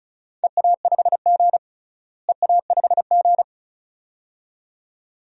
Morse code, heard as a keyed tone, about 36 wpm, 700 Hz. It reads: EA5G EA5G
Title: DXCC-100 Call Signs - 4 Characters - Repeated 2x - 35wpm